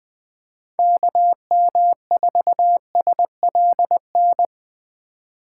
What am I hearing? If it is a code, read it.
Morse code, heard as a keyed tone, about 21 wpm, 700 Hz. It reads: KM4SLN
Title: Speed Racing - USA 2x3 Call Signs 17wpm